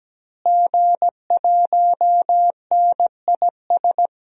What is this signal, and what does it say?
Morse code, heard as a keyed tone, about 17 wpm, 700 Hz. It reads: G1NIS
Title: DXCC-100 Call Signs - 5 Characters - Repeated 3x - 17wpm